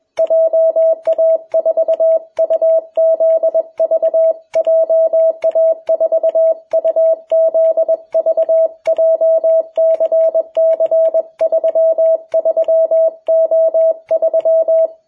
Morse code, heard as a keyed tone, about 21 wpm, 660 Hz, with radio noise. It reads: JA4UZV JA4UZV JCC33O3
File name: jh8cbh / 2021cw